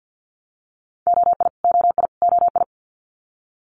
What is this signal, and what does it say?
Morse code, heard as a keyed tone, about 50 wpm, 690 Hz. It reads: OI OI OI
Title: Multi-letter Phonemes Repeated 3x 50wpm